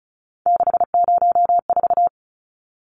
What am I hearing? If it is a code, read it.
Morse code, heard as a keyed tone, about 34 wpm, 700 Hz. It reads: 604
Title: Numbers - 3-Digit 35wpm